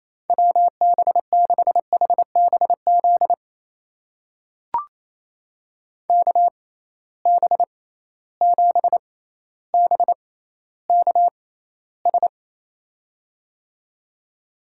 Morse code, heard as a keyed tone, about 28 wpm, spaced as for 14 wpm, 700 Hz. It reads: WB6HBZ KB7BKS
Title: USA Call Signs - 6 Characters - 6x Character Spacing - 28wpm